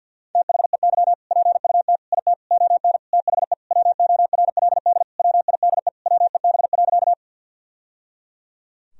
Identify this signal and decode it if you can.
Morse code, heard as a keyed tone, about 52 wpm, 700 Hz. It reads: THEY PUT IT ON THE WORLD WIDE WEB.
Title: Sentences from Top 1000 Words - Review Part 4 - 50wpm